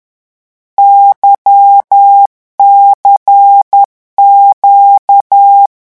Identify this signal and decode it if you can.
Morse code, heard as a keyed tone, about 10 wpm, 800 Hz. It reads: YCQ
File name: morse mp3 / lecon2-vitesse 10